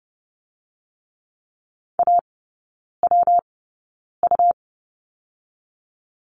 Morse code, heard as a keyed tone, about 30 wpm, 700 Hz. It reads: A W U
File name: Lesson 19e - Letter w - Three Characters 30wpm